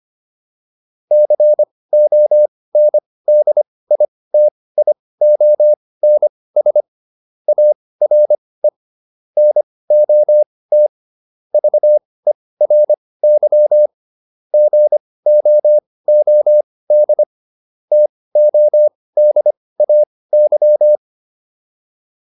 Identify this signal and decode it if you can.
Morse code, heard as a keyed tone, about 25 wpm, spaced as for 18 wpm, 600 Hz. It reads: CONDITIONS ARE NOT VERY GOOD TODAY
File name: QSO407 18